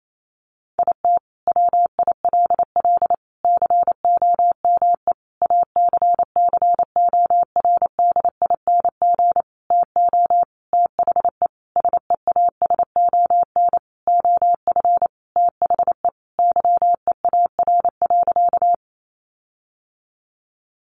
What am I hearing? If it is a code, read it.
Morse code, heard as a keyed tone, about 28 wpm, 700 Hz. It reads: IT WILL COME ACCORDING TO THE SEASON OF THE YEAR.
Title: Sentences from Top 1400 Words 28wpm